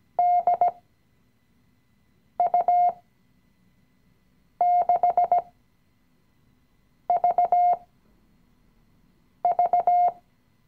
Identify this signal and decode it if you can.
Morse code, heard as a keyed tone, about 17 wpm, spaced as for 5 wpm, 710 Hz. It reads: DU6VV
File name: CW-Online-Kurs / Anfänger / Lektion 018